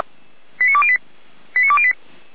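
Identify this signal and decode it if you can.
Five-tone selective call (ZVEI), heard as quick runs of five tones, tones of about 70 ms. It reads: 89208 89208
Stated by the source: ZVEI five-tone call received by SDR.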